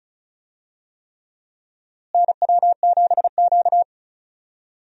Morse code, heard as a keyed tone, about 34 wpm, 700 Hz. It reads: NW7Q
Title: USA Call Signs - 4 Characters - Repeated 3x - 35wpm